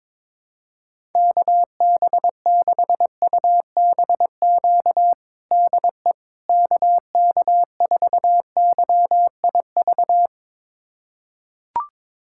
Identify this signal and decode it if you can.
Morse code, heard as a keyed tone, about 22 wpm, 700 Hz. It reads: KB6UBQ DE KK4YIV
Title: USA Callsign DE Callsign - 6 Characters - 22wpm